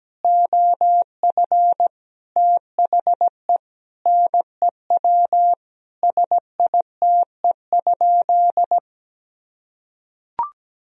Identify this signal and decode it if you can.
Morse code, heard as a keyed tone, about 17 wpm, 700 Hz. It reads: OF THE NEW SITE?
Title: Sentences from Top 1000 Words - Review Part 4 - 17wpm